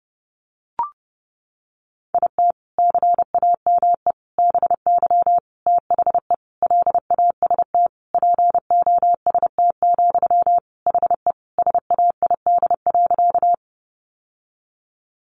Morse code, heard as a keyed tone, about 30 wpm, 700 Hz. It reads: IT CAME BY THE LAST POST, HE SAID.
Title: Sentences from Top 1000 Words 30wpm